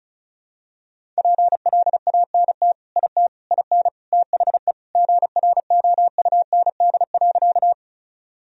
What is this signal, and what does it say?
Morse code, heard as a keyed tone, about 36 wpm, 700 Hz. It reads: PLANT IT IN THE GROUND.
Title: Sentences from Top 1000 Words - Review Part 1 - 35wpm